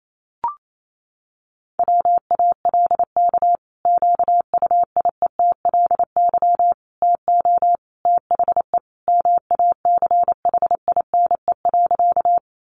Morse code, heard as a keyed tone, about 28 wpm, 700 Hz. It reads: WALK QUIETLY TO THE MACHINE.
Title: A Sound Of Thunder 28wpm